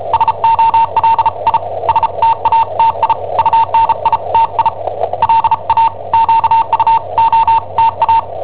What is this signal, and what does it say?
Morse code, heard as a keyed tone, about 34 wpm, spaced as for 30 wpm, 910 Hz, with radio noise. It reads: SOLI STATI UNITI LA QUOTA
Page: Cassetta esercizi preparata da I7OHP per IW5ELP